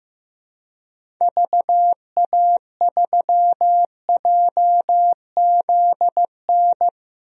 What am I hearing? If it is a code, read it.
Morse code, heard as a keyed tone, about 15 wpm, 700 Hz. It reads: VA3JZN
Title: DXCC-100 Call Signs - 6 Characters - 15wpm